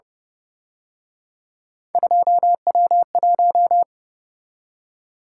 Morse code, heard as a keyed tone, about 29 wpm, 700 Hz. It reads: 2W1